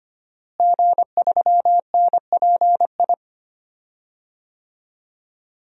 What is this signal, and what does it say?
Morse code, heard as a keyed tone, about 25 wpm, 700 Hz. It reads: G3NPI
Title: DXCC-100 Call Signs - 5 Characters - Repeated 2x - 25wpm